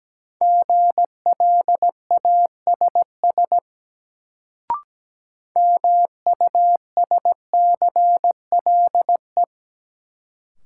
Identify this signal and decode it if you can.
Morse code, heard as a keyed tone, about 17 wpm, 700 Hz. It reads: GLASS MUSCLE
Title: Sentences from Top 1300 Words - New Word Drill Alt - 17wpm